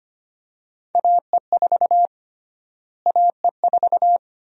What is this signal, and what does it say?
Morse code, heard as a keyed tone, about 25 wpm, 700 Hz. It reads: AE4 AE4
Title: USA Call Sign Prefixes - Repeated 2x - 25wpm